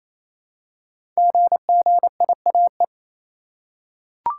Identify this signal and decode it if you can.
Morse code, heard as a keyed tone, about 28 wpm, 700 Hz. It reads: GGIAE
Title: Code Groups - 5 Letters - 28wpm